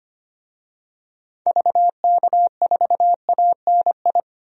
Morse code, heard as a keyed tone, about 25 wpm, 700 Hz. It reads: VK4ANI
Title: DXCC-100 Call Signs - 6 Characters - Repeated 2x - 25wpm